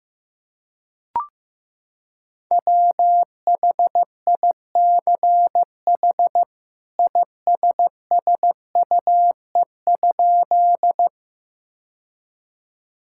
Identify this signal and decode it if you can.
Morse code, heard as a keyed tone, about 16 wpm, 700 Hz. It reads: WHICH ISSUE?
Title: Sentences from Top 1000 Words - Review Part 6 - 15wpm